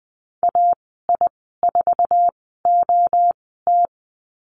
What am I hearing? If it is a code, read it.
Morse code, heard as a keyed tone, about 20 wpm, spaced as for 14 wpm, 700 Hz. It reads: AI4OT
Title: 055 Lesson 10g - Letter r - Call Signs 20wpm